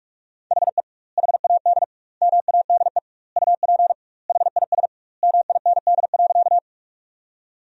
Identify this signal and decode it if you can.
Morse code, heard as a keyed tone, about 46 wpm, 700 Hz. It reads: HE HAD MADE UP HIS MIND.
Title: 2x Word-Spacing - Sentences from Top 500 Words - 45wpm